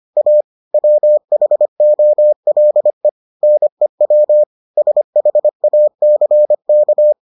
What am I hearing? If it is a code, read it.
Morse code, heard as a keyed tone, about 25 wpm, 600 Hz. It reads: A WHOLE NEW SHACK